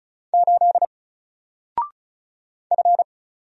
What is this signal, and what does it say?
Morse code, heard as a keyed tone, about 35 wpm, 700 Hz. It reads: 8 F
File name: Single-Letter-Number-35wpm